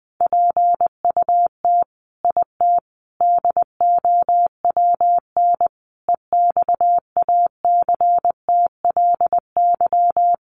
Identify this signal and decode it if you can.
Morse code, heard as a keyed tone, about 20 wpm, 700 Hz. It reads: PUT IT DOWN EXACTLY
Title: Sentences from Top 900 Words 20wpm